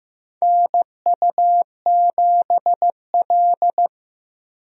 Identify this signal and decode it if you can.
Morse code, heard as a keyed tone, about 15 wpm, 700 Hz. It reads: NU7L
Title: USA Call Signs - 4 Characters - Repeated 3x - 15wpm